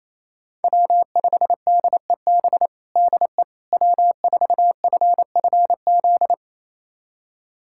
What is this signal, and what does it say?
Morse code, heard as a keyed tone, about 28 wpm, 700 Hz. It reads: W5DEB DE W4FFZ
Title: USA Callsign DE Callsign - 5 Characters - 28wpm